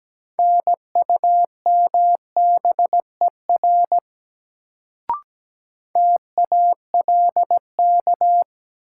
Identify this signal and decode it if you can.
Morse code, heard as a keyed tone, about 17 wpm, 700 Hz. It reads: NUMBER TALK
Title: Sets of 2 Words - New Word Drill - 17wpm